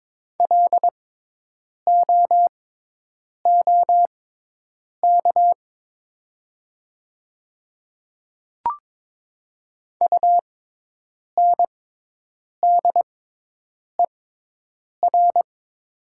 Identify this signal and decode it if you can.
Morse code, heard as a keyed tone, about 22 wpm, spaced as for 8 wpm, 700 Hz. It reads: LOOK UNDER
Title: 6x Character Spacing - Top 200 Words - 22wpm